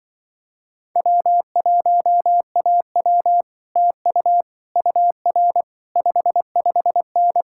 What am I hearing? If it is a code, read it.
Morse code, heard as a keyed tone, about 24 wpm, 700 Hz. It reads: W1AW TU UR 55N
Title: Speed-Racing Common QSO Phrases - 20wpm